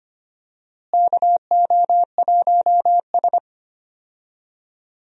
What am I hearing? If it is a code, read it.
Morse code, heard as a keyed tone, about 25 wpm, 700 Hz. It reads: KO1S
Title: USA Call Signs - 4 Characters - Repeated 3x - 25wpm